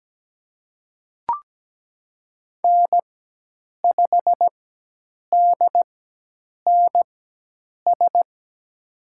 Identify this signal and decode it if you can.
Morse code, heard as a keyed tone, about 17 wpm, spaced as for 8 wpm, 700 Hz. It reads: N5DNS
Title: USA Call Signs - 4x Character Spacing - 17wpm